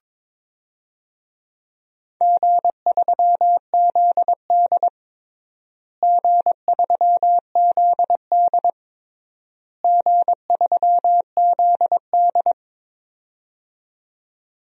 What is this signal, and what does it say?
Morse code, heard as a keyed tone, about 22 wpm, 700 Hz. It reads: G3ZD G3ZD G3ZD
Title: DXCC-100 Call Signs - 4 Characters - Repeated 3x - 22wpm